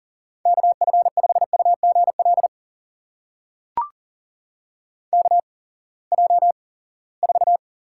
Morse code, heard as a keyed tone, about 40 wpm, spaced as for 23 wpm, 700 Hz. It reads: KF5UGL KJ4
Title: USA Call Signs - 6 Characters - 8x Character Spacing - 40wpm